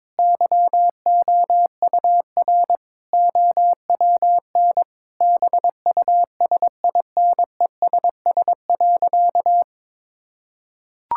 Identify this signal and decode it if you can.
Morse code, heard as a keyed tone, about 22 wpm, 700 Hz. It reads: YOUR OWN BUSINESS.
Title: Sentences from Top 1000 Words - Review Part 5 - 22wpm